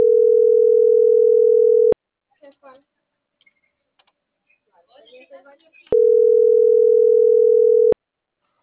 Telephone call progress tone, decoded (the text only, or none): ringback tone